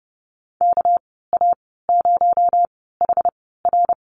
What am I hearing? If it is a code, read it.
Morse code, heard as a keyed tone, about 30 wpm, 700 Hz. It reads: K A 0 H R
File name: Lesson 37g - Letter x - Call Signs 30wpm